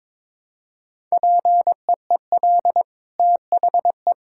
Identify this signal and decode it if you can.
Morse code, heard as a keyed tone, about 22 wpm, 700 Hz. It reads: PEEL THE
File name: practice / 22wpm / Easy Sentences with Hints - 22wpm